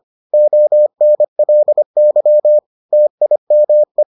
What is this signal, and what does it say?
Morse code, heard as a keyed tone, about 25 wpm, 600 Hz. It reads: ONLY TIME